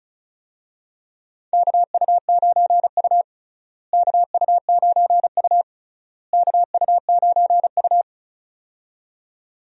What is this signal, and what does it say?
Morse code, heard as a keyed tone, about 36 wpm, 700 Hz. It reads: KU9U KU9U KU9U
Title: USA Call Signs - 4 Characters - Repeated 3x - 35wpm